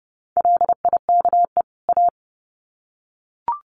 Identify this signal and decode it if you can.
Morse code, heard as a keyed tone, about 30 wpm, 700 Hz. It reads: LIKE A
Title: Sets of 2 Words 30wpm